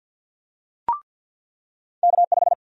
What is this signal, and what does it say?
Morse code, heard as a keyed tone, about 53 wpm, 700 Hz. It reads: K5
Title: USA Call Sign Prefixes - 50wpm